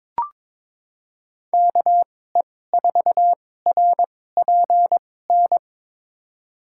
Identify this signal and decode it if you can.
Morse code, heard as a keyed tone, about 22 wpm, spaced as for 16 wpm, 700 Hz. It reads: KE4RPN
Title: USA Call Signs - 6 Characters - 2x Character Spacing - 22wpm